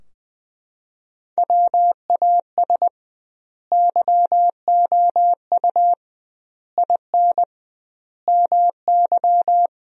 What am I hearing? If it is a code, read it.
Morse code, heard as a keyed tone, about 21 wpm, 700 Hz. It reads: WAS YOU IN MY
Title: 2x Word-Spacing - Sentences from Top 300 Words - 20wpm